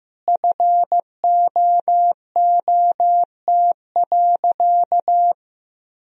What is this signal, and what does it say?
Morse code, heard as a keyed tone, about 16 wpm, 700 Hz. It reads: FOOT.
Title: Sentences from Top 1000 Words - Review Part 3 - 15wpm